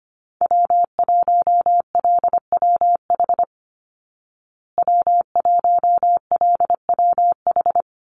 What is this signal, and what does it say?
Morse code, heard as a keyed tone, about 25 wpm, 700 Hz. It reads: W1LWH W1LWH
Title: Call Signs Repeated 3x - 25wpm